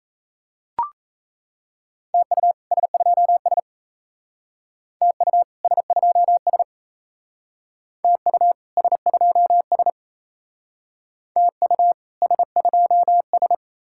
Morse code, heard as a keyed tone, about 32 wpm, 700 Hz. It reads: TU S2S TU S2S TU S2S TU S2S
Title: Speed-Racing Common QSO Phrases - 28wpm